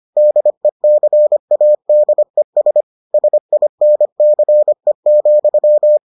Morse code, heard as a keyed tone, about 25 wpm, 600 Hz. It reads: DECADES SINCE,